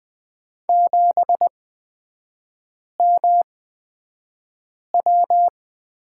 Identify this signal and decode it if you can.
Morse code, heard as a keyed tone, about 20 wpm, spaced as for 5 wpm, 700 Hz. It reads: 7MW
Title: 229 Lesson 37e - Letter x - Three Characters 20wpm